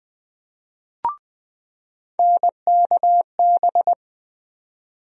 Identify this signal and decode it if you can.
Morse code, heard as a keyed tone, about 20 wpm, 700 Hz. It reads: NKB
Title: USA Call Sign Postfixes - 3 Letters 20wpm